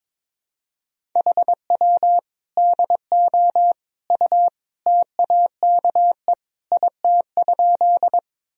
Morse code, heard as a keyed tone, about 22 wpm, 700 Hz. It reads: HW DO U TAKE IT?
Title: Sentences from Top 200 Words and Abbreviations 22wpm